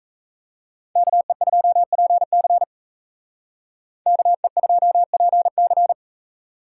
Morse code, heard as a keyed tone, about 36 wpm, 700 Hz. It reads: KE2PC KE2PC